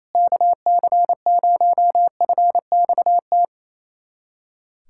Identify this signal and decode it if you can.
Morse code, heard as a keyed tone, about 28 wpm, 700 Hz. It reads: KC0FXT
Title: USA Callsign DE Callsign - 6 Characters - 28wpm